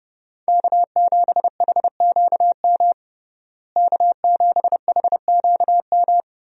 Morse code, heard as a keyed tone, about 30 wpm, 700 Hz. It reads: K7HQM K7HQM
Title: USA Call Signs - 5 Characters - Repeated 2x - 30wpm